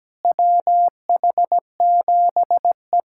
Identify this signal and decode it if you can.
Morse code, heard as a keyed tone, about 17 wpm, 700 Hz. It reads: WH7E
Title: USA Callsign DE Callsign - 4 Characters - 17wpm